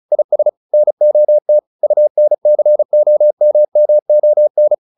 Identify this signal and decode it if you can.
Morse code, heard as a keyed tone, about 34 wpm, 600 Hz. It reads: IS NOT UNCOMMON
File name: SS309 35